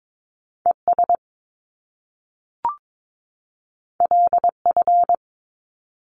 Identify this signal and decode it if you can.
Morse code, heard as a keyed tone, about 22 wpm, 700 Hz. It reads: ES LF
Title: Top Two Letter Combinations in Words 22wpm